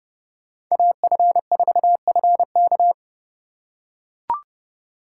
Morse code, heard as a keyed tone, about 30 wpm, 700 Hz. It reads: AF4FK